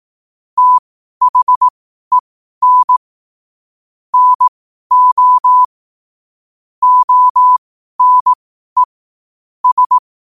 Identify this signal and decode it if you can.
Morse code, heard as a keyed tone, about 18 wpm, spaced as for 12 wpm, 1000 Hz. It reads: THEN NO ONE S
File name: qotdfiles / 12wpm